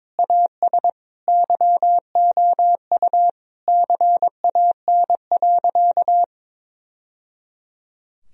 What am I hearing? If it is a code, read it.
Morse code, heard as a keyed tone, about 22 wpm, 700 Hz. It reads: AS YOU CAN.
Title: Sentences from Top 1000 Words - Review Part 1 - 22wpm